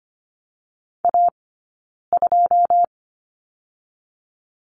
Morse code, heard as a keyed tone, about 25 wpm, spaced as for 9 wpm, 700 Hz. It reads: A2
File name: Lesson 17d - Letter u - Two Characters 25wpm